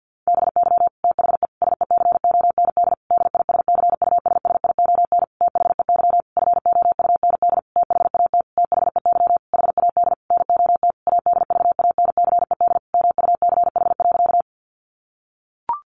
Morse code, heard as a keyed tone, about 50 wpm, 700 Hz. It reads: BY THE SECOND DISCUSSION THEY FOUND THAT THEY HAD NOT ADVANCED MUCH.
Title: Sentences from Top 1200 Words 50wpm